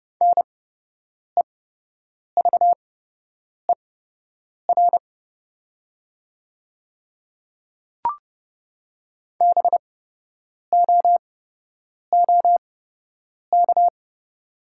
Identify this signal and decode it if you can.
Morse code, heard as a keyed tone, about 30 wpm, spaced as for 8 wpm, 700 Hz. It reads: NEVER BOOK
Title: 8x Character Spacing - Top 300 Words - 30wpm